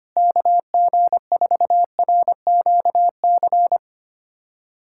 Morse code, heard as a keyed tone, about 25 wpm, 700 Hz. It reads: KG4RQC
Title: USA Callsign DE Callsign 25wpm